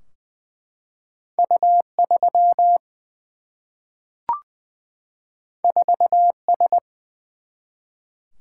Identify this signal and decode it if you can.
Morse code, heard as a keyed tone, about 20 wpm, 700 Hz. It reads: U 3 4 S
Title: Single-Letter Single-Number 20wpm